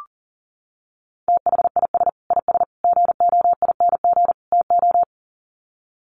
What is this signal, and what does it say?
Morse code, heard as a keyed tone, about 40 wpm, 700 Hz. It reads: THIS IS GOING TO
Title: Sets of 4 Words 40wpm